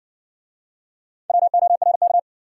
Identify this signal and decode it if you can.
Morse code, heard as a keyed tone, about 60 wpm, 700 Hz. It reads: WORK